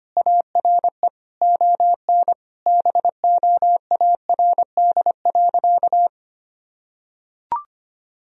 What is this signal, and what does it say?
Morse code, heard as a keyed tone, about 25 wpm, 700 Hz. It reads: ARE ON BOARD.
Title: Sentences from Top 1000 Words - Review Part 4 - 25wpm